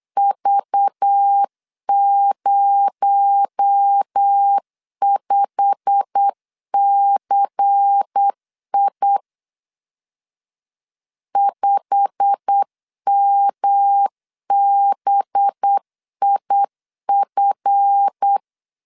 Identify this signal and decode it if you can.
Morse code, heard as a keyed tone, about 8 wpm, 800 Hz. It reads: V05CI 5MBIF
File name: LEZIONI / 8 YJ45 / 8 YJ45 8wpm 3214